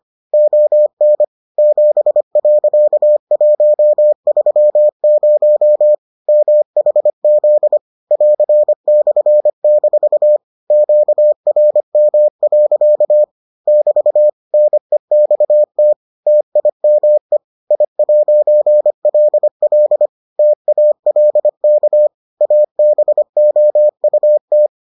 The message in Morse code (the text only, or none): ON 7.130 MHZ <AR>/- QRM. <BT> NEXT TIME I'LL TALK ABOUT